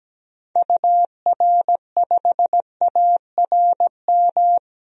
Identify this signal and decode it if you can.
Morse code, heard as a keyed tone, about 17 wpm, 700 Hz. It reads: UR5ARM
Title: DXCC-100 Call Signs - 6 Characters - 17wpm